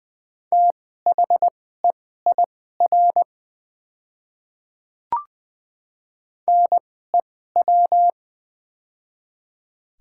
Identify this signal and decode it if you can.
Morse code, heard as a keyed tone, about 20 wpm, spaced as for 14 wpm, 700 Hz. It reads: THEIR NEW
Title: 2x Character Spacing - Top 200 Words - 20wpm